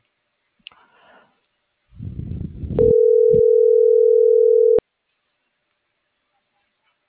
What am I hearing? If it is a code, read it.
Ringback tone (North America), 440 + 480 Hz.